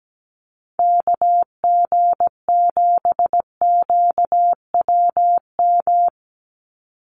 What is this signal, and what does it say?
Morse code, heard as a keyed tone, about 17 wpm, 700 Hz. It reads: KG7QWM
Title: USA Call Signs 17wpm